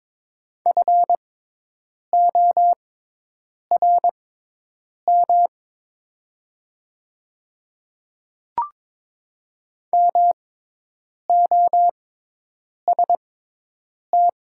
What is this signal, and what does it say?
Morse code, heard as a keyed tone, about 22 wpm, spaced as for 8 wpm, 700 Hz. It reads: FORM MOST
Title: 6x Character Spacing - Top 200 Words - 22wpm